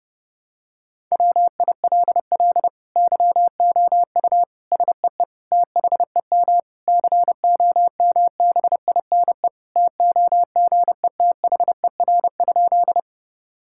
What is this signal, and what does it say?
Morse code, heard as a keyed tone, about 30 wpm, 700 Hz. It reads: WILL YOU SEE THEM COMBINE TOGETHER?
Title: Sentences from Top 1600 Words 30wpm